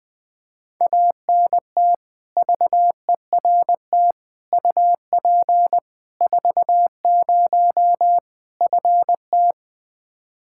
Morse code, heard as a keyed tone, about 21 wpm, 700 Hz. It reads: ANT VERT UP 40 FT
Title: Practice QSO - Exchange 2 - Info - 20wpm